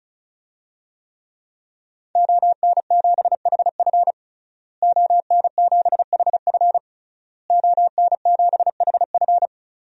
Morse code, heard as a keyed tone, about 36 wpm, 700 Hz. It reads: ON7HF ON7HF ON7HF
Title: DXCC-100 Call Signs - 5 Characters - Repeated 3x - 35wpm